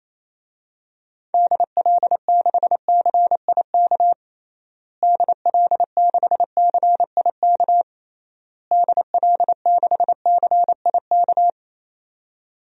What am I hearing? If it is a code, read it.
Morse code, heard as a keyed tone, about 28 wpm, 700 Hz. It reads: DL6CIK DL6CIK DL6CIK
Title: DXCC-100 Call Signs - 6 Characters - Repeated 3x - 28wpm